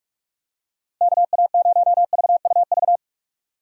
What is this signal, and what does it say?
Morse code, heard as a keyed tone, about 46 wpm, 700 Hz. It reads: KA0VUV
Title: USA Call Signs - 6 Characters - Repeated 2x - 45wpm